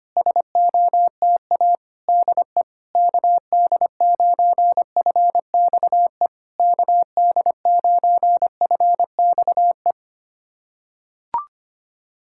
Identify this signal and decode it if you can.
Morse code, heard as a keyed tone, about 25 wpm, 690 Hz. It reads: SOTA DE KD9FXE KD9FXE
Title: SOTA QSO Practice 25wpm